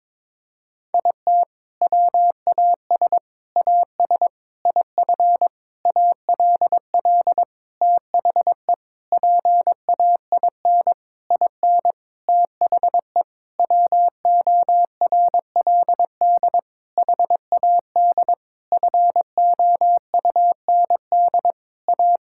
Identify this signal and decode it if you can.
Morse code, heard as a keyed tone, about 22 wpm, 700 Hz. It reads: IT WAS AS IF ALL THE PAIN IN THE WORLD HAD FOUND A
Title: Sentences from Top 1600 Words 22wpm